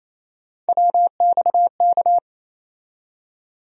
Morse code, heard as a keyed tone, about 28 wpm, 700 Hz. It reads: WXK
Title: USA Call Sign Postfixes - 3 Letters 28wpm